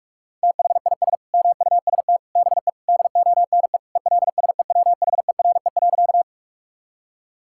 Morse code, heard as a keyed tone, about 43 wpm, 700 Hz. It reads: THIS MUST BE DONE ELSEWHERE.
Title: Sentences from Top 1600 Words 45wpm